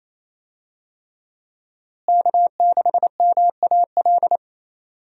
Morse code, heard as a keyed tone, about 28 wpm, 700 Hz. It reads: K6MAL
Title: USA Call Signs - 5 Characters - Repeated 3x - 28wpm